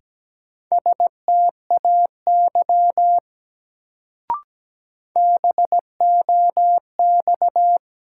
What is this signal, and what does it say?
Morse code, heard as a keyed tone, about 17 wpm, 700 Hz. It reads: STAY BOX
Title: Sentences from Top 500 Words - New Word Drill - 17wpm